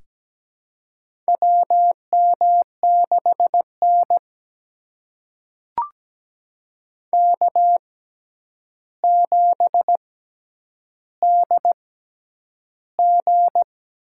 Morse code, heard as a keyed tone, about 17 wpm, spaced as for 9 wpm, 700 Hz. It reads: WM6N K7DG
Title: USA Call Signs - 4 Characters - 6x Character Spacing - 17wpm